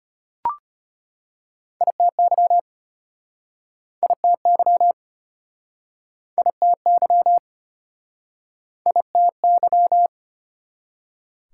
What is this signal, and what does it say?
Morse code, heard as a keyed tone, about 36 wpm, 700 Hz. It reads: ITY ITY ITY ITY